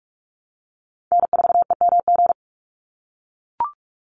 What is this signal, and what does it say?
Morse code, heard as a keyed tone, about 44 wpm, 700 Hz. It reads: N4EMG
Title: NA SOTA Chaser Callsigns 45wpm